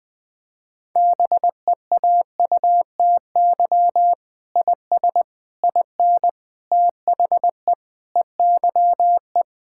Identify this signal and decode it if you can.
Morse code, heard as a keyed tone, about 20 wpm, 700 Hz. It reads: BEAUTY IS IN THE EYE
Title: Sentences with 8-Word Phrases - 20wpm